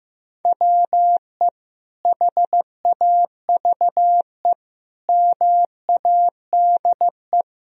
Morse code, heard as a keyed tone, about 15 wpm, 700 Hz. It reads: WE HAVE MADE
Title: Sentences from Top 1000 Words - Review Part 3 - 15wpm